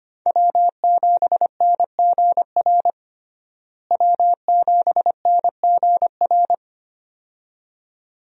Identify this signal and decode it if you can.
Morse code, heard as a keyed tone, about 25 wpm, 700 Hz. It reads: W7NGR W7NGR
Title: USA Call Signs - 5 Characters - Repeated 2x - 25wpm